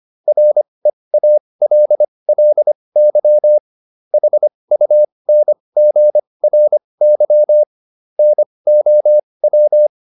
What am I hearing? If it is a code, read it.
Morse code, heard as a keyed tone, about 25 wpm, spaced as for 20 wpm, 600 Hz. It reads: REALLY HUNGRY NOW